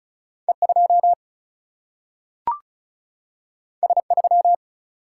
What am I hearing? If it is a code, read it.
Morse code, heard as a keyed tone, about 36 wpm, 700 Hz. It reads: E2 S3
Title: Single-Letter Single-Number 35wpm